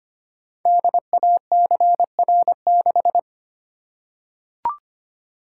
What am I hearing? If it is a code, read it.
Morse code, heard as a keyed tone, about 25 wpm, 700 Hz. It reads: DACR6
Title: Code Groups - 5 Letter:Numbers - 25wpm